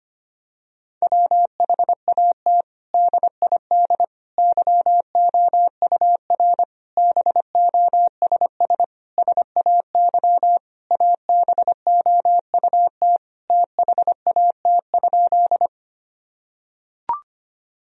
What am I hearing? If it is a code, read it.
Morse code, heard as a keyed tone, about 25 wpm, 700 Hz. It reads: WHAT DID YOUR BOSS SAY ABOUT THAT?
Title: Sentences from Top 1000 Words - Review Part 4 - 25wpm